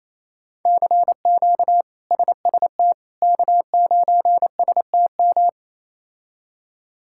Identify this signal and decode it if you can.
Morse code, heard as a keyed tone, about 28 wpm, 700 Hz. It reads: CQ SST K9STM
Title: K1USN SST QSO Practice - 28wpm